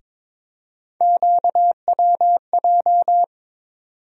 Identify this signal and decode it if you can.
Morse code, heard as a keyed tone, about 22 wpm, 700 Hz. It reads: QWJ
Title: USA Call Sign Postfixes - 3 Letters 22wpm